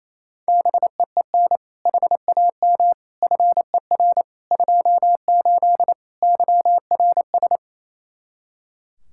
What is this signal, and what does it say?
Morse code, heard as a keyed tone, about 28 wpm, 700 Hz. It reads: BEEN HAM FER 28 YRS
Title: Practice QSO - Exchange 3 - More Info - 28wpm